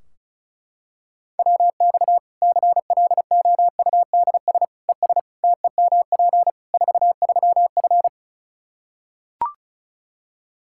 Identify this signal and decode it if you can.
Morse code, heard as a keyed tone, about 34 wpm, 700 Hz. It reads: WX CLOUDS ES TEMP 43F
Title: Practice QSO - Exchange 2 - Info - 35wpm